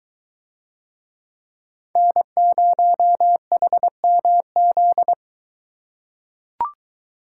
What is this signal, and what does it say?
Morse code, heard as a keyed tone, about 21 wpm, 700 Hz. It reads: N0HMZ